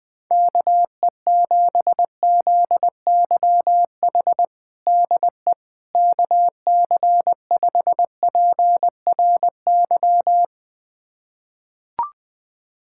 Morse code, heard as a keyed tone, about 21 wpm, 700 Hz. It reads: KE7ZYH DE KC5PRY
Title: USA Callsign DE Callsign - 6 Characters - 20wpm